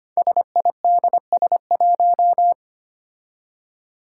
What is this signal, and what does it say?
Morse code, heard as a keyed tone, about 25 wpm, 700 Hz. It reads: SIDS1
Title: Five Challenging Letter-Number Combinations 25wpm